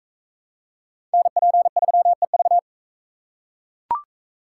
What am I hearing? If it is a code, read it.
Morse code, heard as a keyed tone, about 42 wpm, 700 Hz. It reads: NP3EV